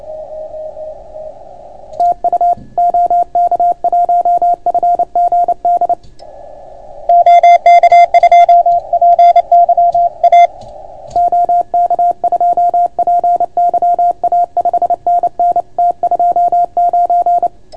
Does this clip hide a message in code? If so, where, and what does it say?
Yes: Morse code from 2.0 to 17.5 s, heard as a keyed tone, about 28 wpm, 690 Hz. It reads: TU OK1FGD OK2PYA OK2PYA 5NN T29